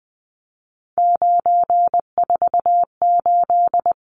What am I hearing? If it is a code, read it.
Morse code, heard as a keyed tone, about 20 wpm, 700 Hz. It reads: 948
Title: Numbers - 3-Digit 20wpm